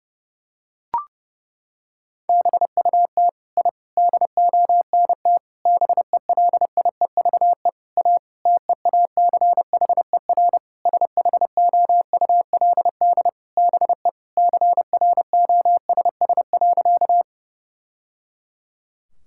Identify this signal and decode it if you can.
Morse code, heard as a keyed tone, about 30 wpm, 700 Hz. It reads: BUT I DONT BELIEVE A TEACHER SHOULD BE CROSS.
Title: Sentences from Top 1000 Words - Review Part 6 - 30wpm